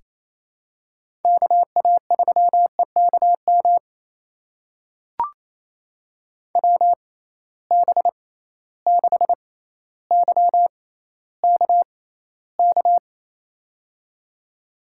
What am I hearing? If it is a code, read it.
Morse code, heard as a keyed tone, about 28 wpm, spaced as for 14 wpm, 700 Hz. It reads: KA3EKM WB6YKK
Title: USA Call Signs - 6 Characters - 6x Character Spacing - 28wpm